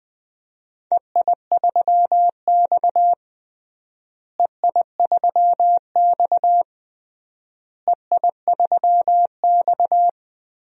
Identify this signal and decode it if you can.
Morse code, heard as a keyed tone, about 20 wpm, 700 Hz. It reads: EI3X EI3X EI3X
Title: DXCC-100 Call Signs - 4 Characters - Repeated 3x - 20wpm